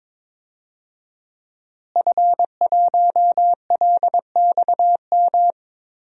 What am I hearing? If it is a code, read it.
Morse code, heard as a keyed tone, about 22 wpm, 700 Hz. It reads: F1LXM
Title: DXCC-100 Call Signs - 5 Characters - Repeated 2x - 22wpm